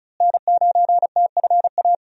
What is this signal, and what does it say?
Morse code, heard as a keyed tone, about 34 wpm, 700 Hz. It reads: N9TFA
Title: USA Call Signs - 5 Characters - Repeated 3x - 35wpm